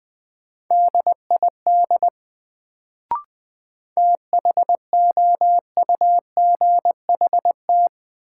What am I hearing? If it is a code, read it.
Morse code, heard as a keyed tone, about 20 wpm, 700 Hz. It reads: DID THOUGHT
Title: Sets of 2 Words - New Word Drill - 20wpm